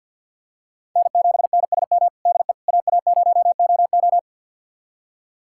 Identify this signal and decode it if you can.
Morse code, heard as a keyed tone, about 50 wpm, 700 Hz. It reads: N7NSM DE AA0OO
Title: USA Callsign DE Callsign - 5 Characters - 50wpm